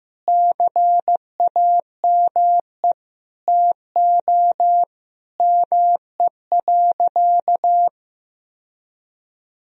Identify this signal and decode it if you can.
Morse code, heard as a keyed tone, about 15 wpm, 700 Hz. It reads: CAME TO ME.
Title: Sentences from Top 1000 Words - Review Part 2 - 15wpm